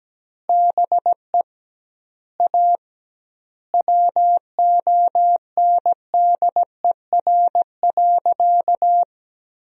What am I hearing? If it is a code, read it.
Morse code, heard as a keyed tone, about 17 wpm, 700 Hz. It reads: BE A WONDER.
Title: 2x Word-Spacing - Sentences from Top 500 Words - 17wpm